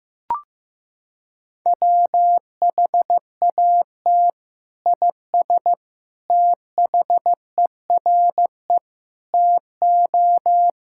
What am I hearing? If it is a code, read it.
Morse code, heard as a keyed tone, about 15 wpm, 700 Hz. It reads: WHAT IS THERE TO
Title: Sentences from Top 1600 Words 15wpm